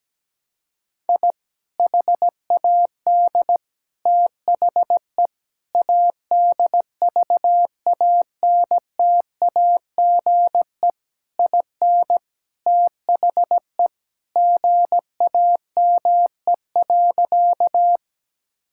Morse code, heard as a keyed tone, about 17 wpm, 700 Hz. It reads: I HAD THE ADVANTAGE IN THE GAME.
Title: Sentences from Top 1000 Words - Review Part 6 - 17wpm